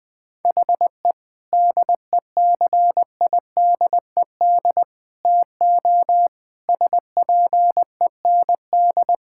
Morse code, heard as a keyed tone, about 21 wpm, 700 Hz. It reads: HE DECIDED TO SPEND
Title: Sentences with 7-Word Phrases - 20wpm